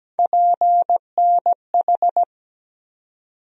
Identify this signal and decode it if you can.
Morse code, heard as a keyed tone, about 17 wpm, 700 Hz. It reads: PNH
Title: USA Call Sign Postfixes - 3 Letters 17wpm